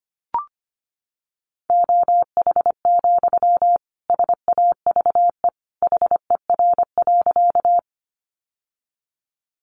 Morse code, heard as a keyed tone, about 25 wpm, 700 Hz. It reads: OH, SAVE HER.
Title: Sentences from Top 1100 Words 25wpm